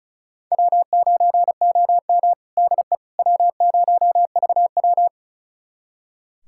Morse code, heard as a keyed tone, about 36 wpm, 700 Hz. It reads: W9OM DE W0VW
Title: USA Callsign DE Callsign - 4 Characters - 35wpm